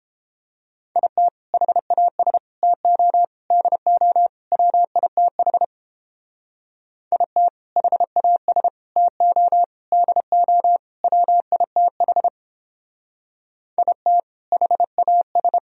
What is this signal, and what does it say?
Morse code, heard as a keyed tone, about 29 wpm, 700 Hz. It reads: IT HAS TO DO WITH IT HAS TO DO WITH IT HAS